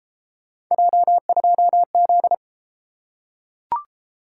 Morse code, heard as a keyed tone, about 32 wpm, 700 Hz. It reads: J2Z